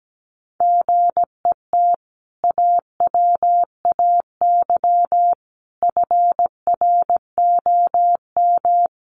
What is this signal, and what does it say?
Morse code, heard as a keyed tone, about 17 wpm, 700 Hz. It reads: GET AWAY FROM
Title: Sentences from Top 600 Words 17wpm